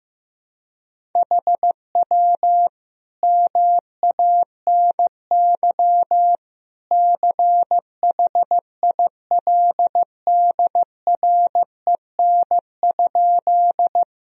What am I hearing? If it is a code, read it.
Morse code, heard as a keyed tone, about 15 wpm, 700 Hz. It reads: HW MANY CHILDREN?
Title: Sentences from Top 300 Words and Abbreviations 15wpm